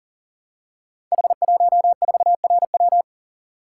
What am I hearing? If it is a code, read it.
Morse code, heard as a keyed tone, about 40 wpm, 690 Hz. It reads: H14RW
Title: Code Groups - 5 Letter:Numbers - 40wpm